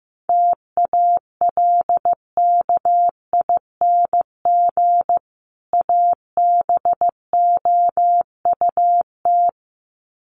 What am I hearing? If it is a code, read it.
Morse code, heard as a keyed tone, about 15 wpm, 700 Hz. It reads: TALKING ABOUT
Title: Sets of 2 Words 15wpm